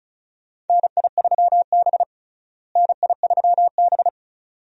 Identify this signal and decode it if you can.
Morse code, heard as a keyed tone, about 36 wpm, 700 Hz. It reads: NI3B NI3B
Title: USA Call Signs - 4 Characters - Repeated 2x - 35wpm